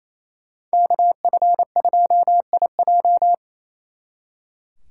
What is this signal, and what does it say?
Morse code, heard as a keyed tone, about 32 wpm, 700 Hz. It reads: KF2IJ
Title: Speed Racing - USA 2x2 Call Signs 28wpm